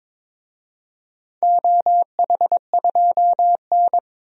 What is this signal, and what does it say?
Morse code, heard as a keyed tone, about 22 wpm, 700 Hz. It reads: OH2N
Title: DXCC-100 Call Signs - 4 Characters - Repeated 2x - 22wpm